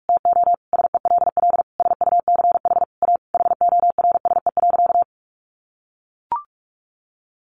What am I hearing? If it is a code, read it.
Morse code, heard as a keyed tone, about 44 wpm, 700 Hz. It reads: TO SELL SUCH A HORSE.
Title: Sentences from Top 1100 Words 45wpm